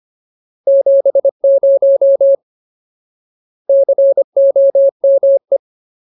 Morse code, heard as a keyed tone, about 25 wpm, 550 Hz. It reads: 70 COME
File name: Ultimate Word List / 25WPM Ultimate